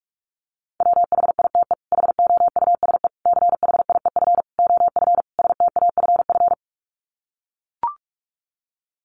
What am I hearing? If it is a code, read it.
Morse code, heard as a keyed tone, about 44 wpm, 700 Hz. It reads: WHITE HOUSE CHIEF OF STAFF
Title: Sets of 5 Words Encore 45wpm